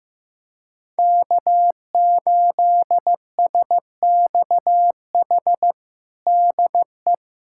Morse code, heard as a keyed tone, about 15 wpm, 700 Hz. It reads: K8SXH DE
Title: USA Callsign DE Callsign 15wpm